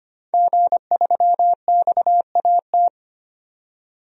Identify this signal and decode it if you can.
Morse code, heard as a keyed tone, about 25 wpm, 700 Hz. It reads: G3XAT
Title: DXCC-100 Call Signs - 5 Characters - 25wpm